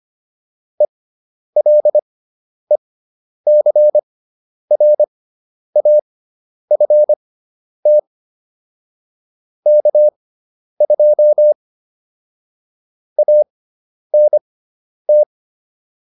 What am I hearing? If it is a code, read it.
Morse code, heard as a keyed tone, about 25 wpm, spaced as for 10 wpm, 600 Hz. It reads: ELECRAFT K2 ANT
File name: QSO404 10